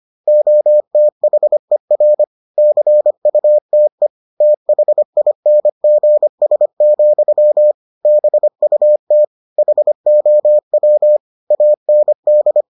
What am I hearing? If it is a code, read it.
Morse code, heard as a keyed tone, about 25 wpm, 600 Hz. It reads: OTHER CUTE THINGS, BUT HOW AND